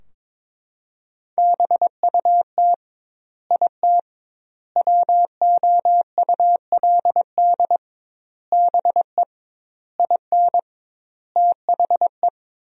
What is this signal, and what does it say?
Morse code, heard as a keyed tone, about 22 wpm, 700 Hz. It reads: BUT IT WOULD BE IN THE
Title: 2x Word-Spacing - Sentences from Top 300 Words - 22wpm